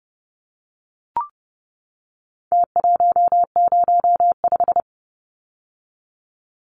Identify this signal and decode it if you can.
Morse code, heard as a keyed tone, about 30 wpm, 700 Hz. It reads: T105
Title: Numbers - Contest Style - 4-Digit 30wpm